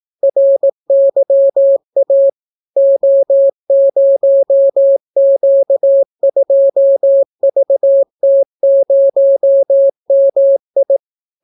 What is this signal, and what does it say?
Morse code, heard as a keyed tone, about 18 wpm, 550 Hz. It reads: RYA O0Q2VT0MI